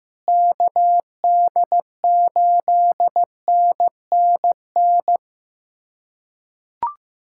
Morse code, heard as a keyed tone, about 15 wpm, 700 Hz. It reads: KD8NNN
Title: USA Callsign DE Callsign - 6 Characters - 15wpm